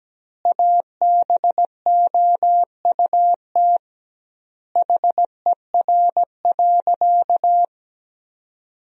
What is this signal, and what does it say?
Morse code, heard as a keyed tone, about 17 wpm, 700 Hz. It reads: ABOUT HER.
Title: 2x Word-Spacing - Sentences from Top 200 Words - 17wpm